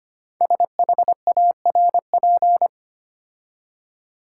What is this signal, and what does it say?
Morse code, heard as a keyed tone, about 25 wpm, 700 Hz. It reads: SHARP
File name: 5 Letter Words - First 3 Letters Given - 25wpm